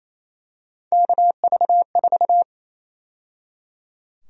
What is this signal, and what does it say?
Morse code, heard as a keyed tone, about 27 wpm, 700 Hz. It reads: KV4
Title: USA Call Sign Prefixes - 28wpm